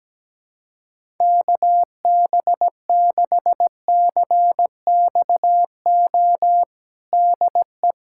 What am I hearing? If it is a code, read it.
Morse code, heard as a keyed tone, about 17 wpm, 700 Hz. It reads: KB6CXO DE
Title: USA Callsign DE Callsign - 6 Characters - 17wpm